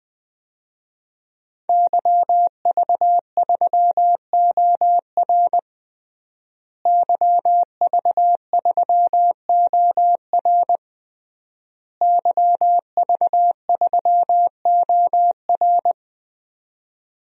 Morse code, heard as a keyed tone, about 20 wpm, 700 Hz. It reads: YV3OR YV3OR YV3OR
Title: DXCC-100 Call Signs - 5 Characters - Repeated 3x - 20wpm